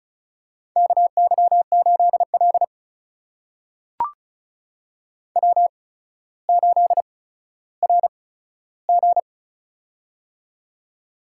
Morse code, heard as a keyed tone, about 36 wpm, spaced as for 15 wpm, 700 Hz. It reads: KY8L W8RG
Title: USA Call Signs - 4 Characters - 8x Character Spacing - 35wpm